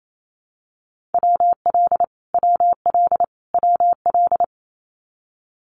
Morse code, heard as a keyed tone, about 28 wpm, 700 Hz. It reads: WL WL WL
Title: Abbreviations Repeated 3x - 28wpm